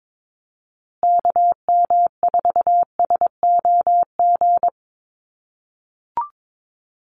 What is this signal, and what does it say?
Morse code, heard as a keyed tone, about 22 wpm, 700 Hz. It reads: KM4SOG
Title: USA Call Signs Repeated 3x 22wpm